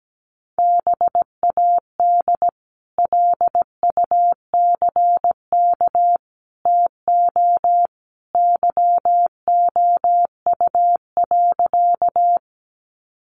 Morse code, heard as a keyed tone, about 17 wpm, 700 Hz. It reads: BAD LUCK TO YOU.
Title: Sentences from Top 1500 Words 17wpm